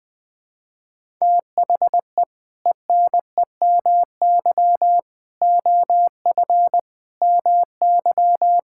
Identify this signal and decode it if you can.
Morse code, heard as a keyed tone, about 20 wpm, 700 Hz. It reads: THE ENEMY OF MY
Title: Sets of 8 Words - 20wpm